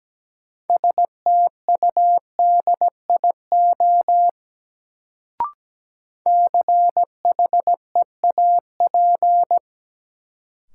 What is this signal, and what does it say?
Morse code, heard as a keyed tone, about 17 wpm, 700 Hz. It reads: STUDIO CHEAP
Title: Sentences from Top 1200 Words - New Word Drill - 17wpm